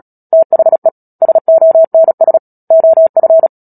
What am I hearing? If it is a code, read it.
Morse code, heard as a keyed tone, about 37 wpm, 650 Hz. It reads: THE SONS OF
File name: Book Two / 10-Chapter 10 - Epilogue - 36 WPM